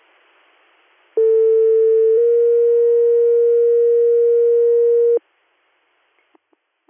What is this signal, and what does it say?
Two-tone page, heard as a one-second tone, then a three-second tone, about 450 then 470 Hz. It is heard from 1.2 to 5.2 s.